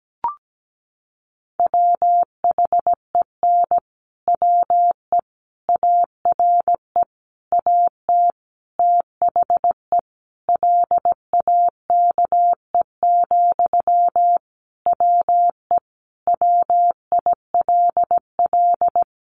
Morse code, heard as a keyed tone, about 17 wpm, 700 Hz. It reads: WHEN WE ARE AT THE LAKE, WE WILL
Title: Sentences from Top 1100 Words 17wpm